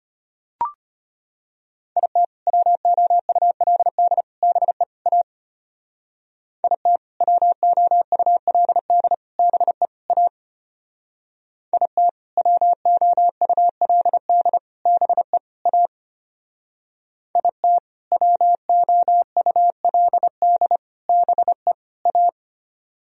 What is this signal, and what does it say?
Morse code, heard as a keyed tone, about 33 wpm, 700 Hz. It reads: IT WOULD BE A IT WOULD BE A IT WOULD BE A IT WOULD BE A